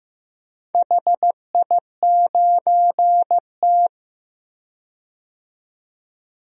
Morse code, heard as a keyed tone, about 15 wpm, 700 Hz. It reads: HI9T
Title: DXCC-100 Call Signs - 4 Characters - 15wpm